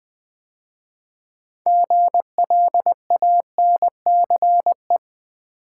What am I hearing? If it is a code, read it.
Morse code, heard as a keyed tone, about 20 wpm, 700 Hz. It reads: GLANCE
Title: 6 Letter Words - First 3 Letters Given - 20wpm